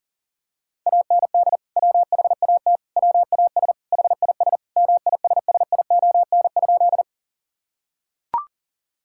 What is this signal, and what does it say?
Morse code, heard as a keyed tone, about 40 wpm, 700 Hz. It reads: AND WHAT WAS HIS MISSION?
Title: Sentences from Top 1600 Words 40wpm